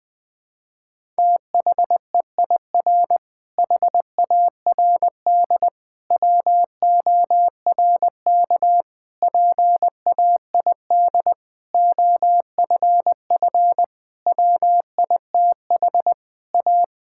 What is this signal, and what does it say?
Morse code, heard as a keyed tone, about 20 wpm, 700 Hz. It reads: THEIR HARD WORK PAID OFF WITH A
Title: Difficult Sentences with Hints - 20wpm